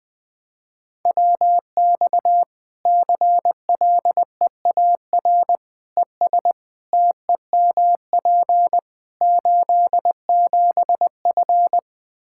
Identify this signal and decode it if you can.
Morse code, heard as a keyed tone, about 21 wpm, 700 Hz. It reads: WX CLEAR ES TEMP 87F
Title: Practice QSO - Exchange 2 - Info - 20wpm